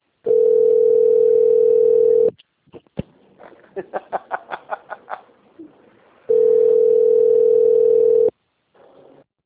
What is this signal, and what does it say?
Ringback tone (North America), 440 + 480 Hz.